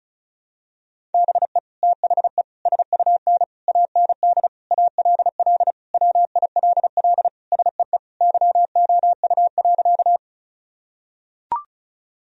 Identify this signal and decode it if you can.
Morse code, heard as a keyed tone, about 36 wpm, 700 Hz. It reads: BE THE SUN AND ALL WILL SEE YOU.
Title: Sentences from Top 1000 Words - Review Part 1 - 35wpm